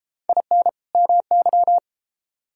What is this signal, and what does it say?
Morse code, heard as a keyed tone, about 32 wpm, 700 Hz. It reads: IN MY